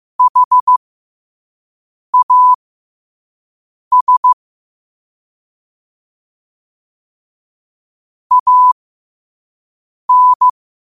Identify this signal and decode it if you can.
Morse code, heard as a keyed tone, about 15 wpm, spaced as for 5 wpm, 1000 Hz. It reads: HAS AN
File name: qotdfiles / 5wpm